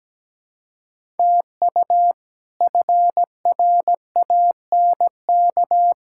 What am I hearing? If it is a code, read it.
Morse code, heard as a keyed tone, about 17 wpm, 700 Hz. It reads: TU FRANK
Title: K1USN SST QSO Practice - 17wpm